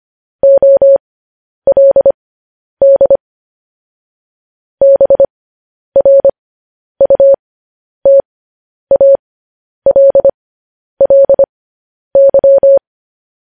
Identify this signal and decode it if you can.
Morse code, heard as a keyed tone, about 25 wpm, spaced as for 10 wpm, 550 Hz. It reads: OLD BRUTALLY
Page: news headlines in morse code at 10 wpm